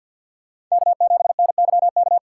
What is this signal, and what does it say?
Morse code, heard as a keyed tone, about 47 wpm, 700 Hz. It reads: K7NYK
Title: USA Call Signs - 5 Characters - 4x Character Spacing - 50wpm